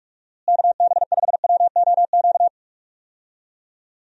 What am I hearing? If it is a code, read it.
Morse code, heard as a keyed tone, about 46 wpm, 700 Hz. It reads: KB5WOQ
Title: USA Call Signs - 6 Characters - Repeated 3x - 45wpm